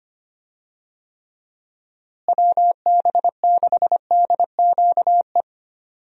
Morse code, heard as a keyed tone, about 25 wpm, 700 Hz. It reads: WB6DQE
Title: USA Call Signs - 6 Characters - Repeated 2x - 25wpm